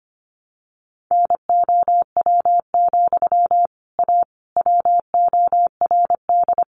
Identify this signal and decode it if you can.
Morse code, heard as a keyed tone, about 25 wpm, 700 Hz. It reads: NOW, A WORD
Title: Sentences from Top 1100 Words 25wpm